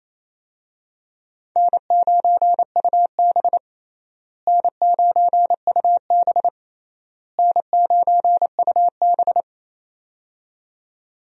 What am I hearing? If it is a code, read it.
Morse code, heard as a keyed tone, about 28 wpm, 700 Hz. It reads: N9UB N9UB N9UB
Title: USA Call Signs - 4 Characters - Repeated 3x - 28wpm